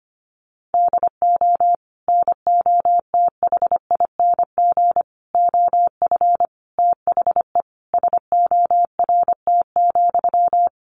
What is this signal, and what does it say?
Morse code, heard as a keyed tone, about 25 wpm, 700 Hz. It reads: DO NOTHING OF THE SORT,
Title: Sentences from Top 1300 Words 25wpm